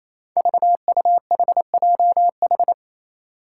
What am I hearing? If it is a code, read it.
Morse code, heard as a keyed tone, about 27 wpm, 700 Hz. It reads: VUHJH
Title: Five Challenging Letter-Number Combinations 28wpm